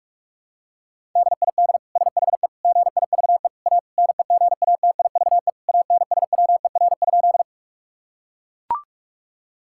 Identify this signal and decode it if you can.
Morse code, heard as a keyed tone, about 41 wpm, 700 Hz. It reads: DID SHE GIVE A NEGATIVE ANSWER?
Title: Sentences from Top 1000 Words - Review Part 6 - 45wpm